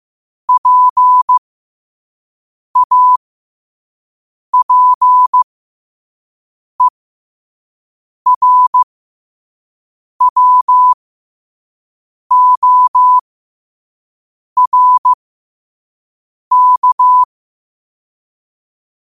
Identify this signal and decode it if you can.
Morse code, heard as a keyed tone, about 15 wpm, spaced as for 5 wpm, 1000 Hz. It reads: PAPERWORK
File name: qotdfiles / 5wpm